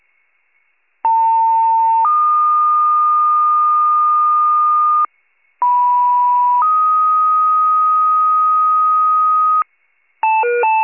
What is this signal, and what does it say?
Two-tone page, heard as a one-second tone, then a three-second tone, about 900 then 1220 Hz, about 980 then 1250 Hz. It is heard from 1.0 to 9.6 s.
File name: pager / 78 RESCUE / wcalert